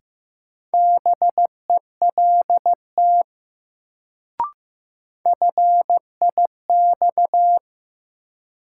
Morse code, heard as a keyed tone, about 15 wpm, 700 Hz. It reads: BELT FIX
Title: Sentences from Top 1400 Words - New Word Drill - 15wpm